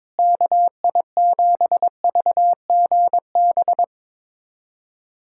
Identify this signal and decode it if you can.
Morse code, heard as a keyed tone, about 22 wpm, 700 Hz. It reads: KI7VGB
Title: USA Callsign DE Callsign - 6 Characters - 22wpm